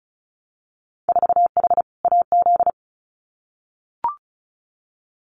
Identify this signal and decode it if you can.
Morse code, heard as a keyed tone, about 36 wpm, 700 Hz. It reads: 4H AZ
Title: Winter Field Day Exchanges Encore 35wpm